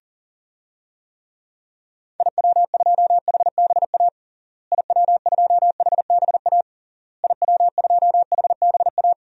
Morse code, heard as a keyed tone, about 40 wpm, 700 Hz. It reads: IW2HBA IW2HBA IW2HBA
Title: DXCC-100 Call Signs - 6 Characters - Repeated 3x - 40wpm